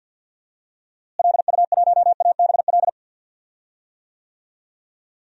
Morse code, heard as a keyed tone, about 51 wpm, 700 Hz. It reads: LU1ABL
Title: DXCC-100 Call Signs - 6 Characters - Repeated 3x - 50wpm